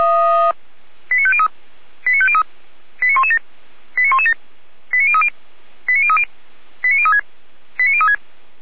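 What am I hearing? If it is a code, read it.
Five-tone selective call (ZVEI), heard as quick runs of five tones, tones of about 70 ms. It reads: 89503 89503 89117 89117 89030 89030 89036 89036
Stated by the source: ZVEI five-tone call received by SDR.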